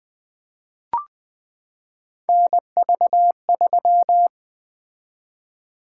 Morse code, heard as a keyed tone, about 20 wpm, 700 Hz. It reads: NV3
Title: Three Letter-Number Combinations - 20wpm